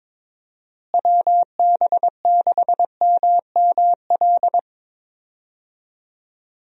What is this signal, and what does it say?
Morse code, heard as a keyed tone, about 22 wpm, 700 Hz. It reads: WB6MML
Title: USA Call Signs - 6 Characters - Repeated 3x - 22wpm